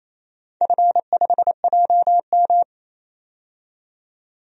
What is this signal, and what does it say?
Morse code, heard as a keyed tone, about 27 wpm, 690 Hz. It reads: F5JM
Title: DXCC-100 Call Signs - 4 Characters - 28wpm